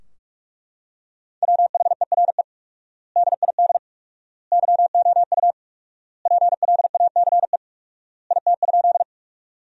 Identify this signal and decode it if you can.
Morse code, heard as a keyed tone, about 43 wpm, 700 Hz. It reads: WHERE DID YOU PLACE IT?
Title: 4x Word-Spacing - Sentences from Top 200 Words - 45wpm